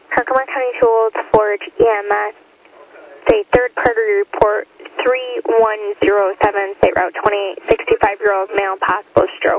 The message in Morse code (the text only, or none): none